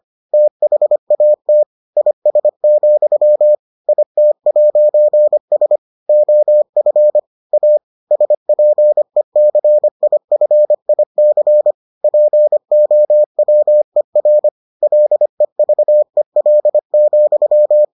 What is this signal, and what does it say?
Morse code, heard as a keyed tone, about 25 wpm, 600 Hz. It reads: THAT IS, IT'S OF A SPECIFIC POWER LEVEL,